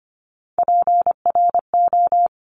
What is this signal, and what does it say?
Morse code, heard as a keyed tone, about 25 wpm, 700 Hz. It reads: PRO
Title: Top Three Letter Combinations in Words 25wpm